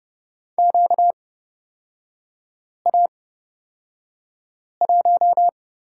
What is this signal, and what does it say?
Morse code, heard as a keyed tone, about 31 wpm, spaced as for 5 wpm, 700 Hz. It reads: QA1
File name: Lesson 38e - Letter q - Three Characters 30wpm